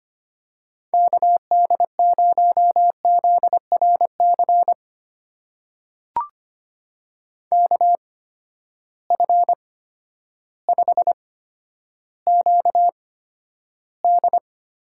Morse code, heard as a keyed tone, about 25 wpm, spaced as for 11 wpm, 700 Hz. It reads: KD0ZRC KF5QD
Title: USA Call Signs - 6 Characters - 8x Character Spacing - 25wpm